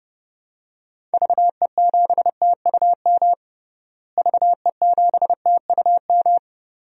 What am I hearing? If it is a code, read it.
Morse code, heard as a keyed tone, about 30 wpm, 700 Hz. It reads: VE7TUM VE7TUM
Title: DXCC-100 Call Signs - 6 Characters - Repeated 2x - 30wpm